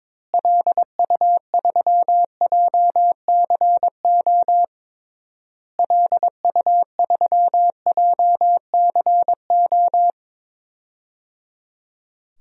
Morse code, heard as a keyed tone, about 22 wpm, 700 Hz. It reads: LU3JCO LU3JCO
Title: DXCC-100 Call Signs - 6 Characters - Repeated 3x - 22wpm